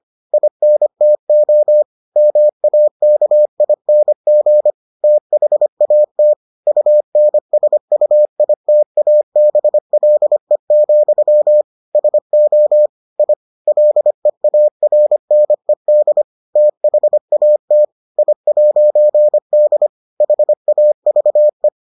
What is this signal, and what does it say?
Morse code, heard as a keyed tone, about 25 wpm, 600 Hz. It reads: INTO MAKING THAT UNSUITABLE, SO I LEARNED THAT I'D HAVE